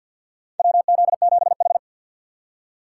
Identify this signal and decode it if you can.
Morse code, heard as a keyed tone, about 51 wpm, 700 Hz. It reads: WZ7H
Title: USA Call Signs - 4 Characters - 6x Character Spacing - 50wpm